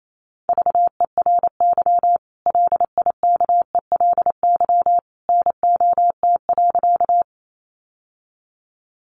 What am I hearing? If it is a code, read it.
Morse code, heard as a keyed tone, about 28 wpm, 700 Hz. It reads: VERY LIKELY NOT.
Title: Sentences from Top 600 Words 28wpm